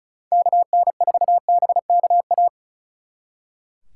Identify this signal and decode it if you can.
Morse code, heard as a keyed tone, about 43 wpm, 700 Hz. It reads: KN4BKA
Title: Speed Racing - USA 2x3 Call Signs 35wpm